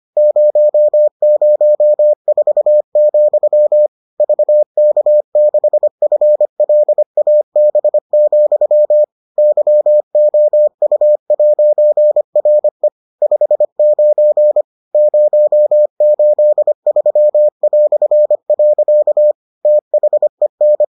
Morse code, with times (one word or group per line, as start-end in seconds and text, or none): 0.2-3.9 004,
4.2-9.0 VK6FLAB,
9.4-12.9 YOU'RE
13.2-14.6 59
14.9-19.3 083".
19.6-20.8 THEN